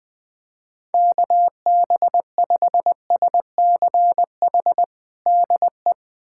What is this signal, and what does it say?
Morse code, heard as a keyed tone, about 20 wpm, 700 Hz. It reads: KB5SCH DE
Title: USA Callsign DE Callsign - 6 Characters - 20wpm